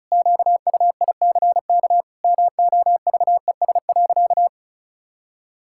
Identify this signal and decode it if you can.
Morse code, heard as a keyed tone, about 34 wpm, 700 Hz. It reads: QUICK MOVES.
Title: Pangram Challenge v2 - 35wpm